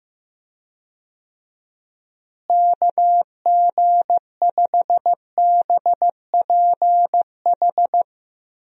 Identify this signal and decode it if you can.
Morse code, heard as a keyed tone, about 15 wpm, 700 Hz. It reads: KG5BPH
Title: USA Call Signs - 6 Characters - Repeated 3x - 15wpm